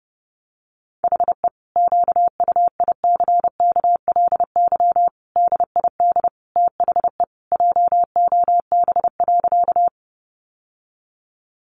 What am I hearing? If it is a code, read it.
Morse code, heard as a keyed tone, about 30 wpm, 700 Hz. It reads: HE QUICKLY DID THE JOB.
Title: Sentences from Top 700 Words 30wpm